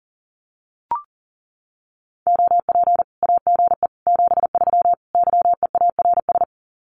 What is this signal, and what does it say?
Morse code, heard as a keyed tone, about 40 wpm, 700 Hz. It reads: OP AGE 73 YEARS
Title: Common QSO Phrases 40wpm